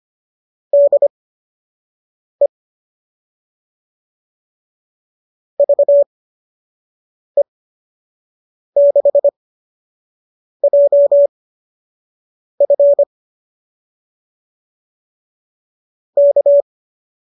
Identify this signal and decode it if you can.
Morse code, heard as a keyed tone, about 25 wpm, spaced as for 6 wpm, 570 Hz. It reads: DE VE6JF K
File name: qso002 06